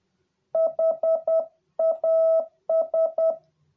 Morse code, heard as a keyed tone, about 10 wpm, 650 Hz. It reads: HAS